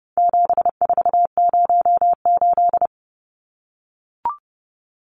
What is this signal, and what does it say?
Morse code, heard as a keyed tone, about 30 wpm, 700 Hz. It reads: 7408
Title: Numbers - 4-Digit 30wpm